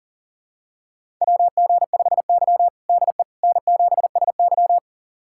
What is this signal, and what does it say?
Morse code, heard as a keyed tone, about 37 wpm, 700 Hz. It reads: WG5Y DE N7SY
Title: USA Callsign DE Callsign - 4 Characters - 40wpm